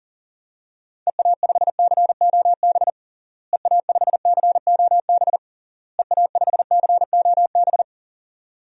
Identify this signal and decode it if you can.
Morse code, heard as a keyed tone, about 40 wpm, 700 Hz. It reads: EA5COB EA5COB EA5COB
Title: DXCC-100 Call Signs - 6 Characters - Repeated 3x - 40wpm